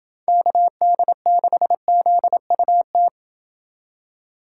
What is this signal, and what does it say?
Morse code, heard as a keyed tone, about 27 wpm, 700 Hz. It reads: KD6ZUT